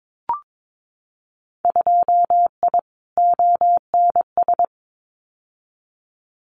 Morse code, heard as a keyed tone, about 22 wpm, 700 Hz. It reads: 2I ONS
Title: Winter Field Day Exchanges Encore 22wpm